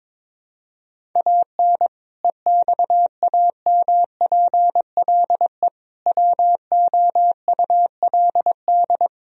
Morse code, heard as a keyed tone, about 22 wpm, 700 Hz. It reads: AN EXAMPLE WOULD
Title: Sentences from Top 1000 Words - Review Part 1 - 22wpm